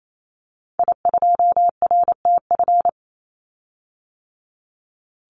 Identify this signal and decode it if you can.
Morse code, heard as a keyed tone, about 28 wpm, 700 Hz. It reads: I2RTF
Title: Call Signs - 28wpm